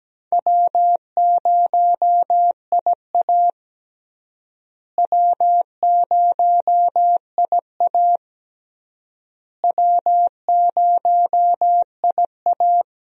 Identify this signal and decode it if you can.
Morse code, heard as a keyed tone, about 17 wpm, 700 Hz. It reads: W0IA W0IA W0IA
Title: USA Call Signs - 4 Characters - Repeated 3x - 17wpm